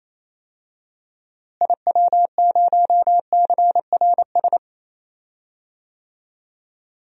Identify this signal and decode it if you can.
Morse code, heard as a keyed tone, about 28 wpm, 700 Hz. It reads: IW0CRS
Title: DXCC-100 Call Signs - 6 Characters - Repeated 3x - 28wpm